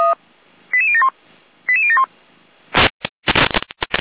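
Five-tone selective call (ZVEI), heard as quick runs of five tones, tones of about 70 ms. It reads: 88071 88071
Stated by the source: ZVEI five-tone selective call received by SDR, archived by Feuerwehr Orlamünde (fire brigade).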